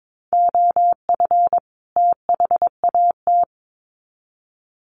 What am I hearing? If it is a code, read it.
Morse code, heard as a keyed tone, about 22 wpm, 700 Hz. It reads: OF THAT
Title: Sets of 2 Words 22wpm